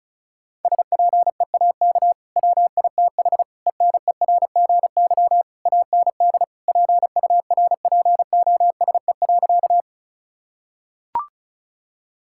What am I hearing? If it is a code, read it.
Morse code, heard as a keyed tone, about 36 wpm, 700 Hz. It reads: SPEAK WITH ENERGY AND PURPOSE.
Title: Sentences from Top 1000 Words - Review Part 6 - 35wpm